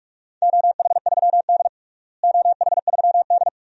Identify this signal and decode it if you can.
Morse code, heard as a keyed tone, about 43 wpm, 700 Hz. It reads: OH3D OH3D
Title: DXCC-100 Call Signs - 4 Characters - Repeated 2x - 45wpm